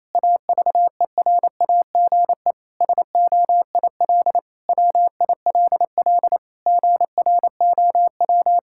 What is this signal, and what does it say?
Morse code, heard as a keyed tone, about 28 wpm, 700 Hz. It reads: AVERAGE SOIL WILL GROW
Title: Sentences from Top 1000 Words - Review Part 6 - 28wpm